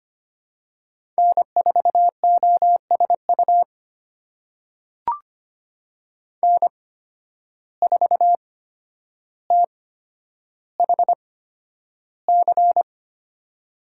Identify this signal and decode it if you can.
Morse code, heard as a keyed tone, about 25 wpm, spaced as for 11 wpm, 700 Hz. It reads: N4OSU N4THC
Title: USA Call Signs - 5 Characters - 8x Character Spacing - 25wpm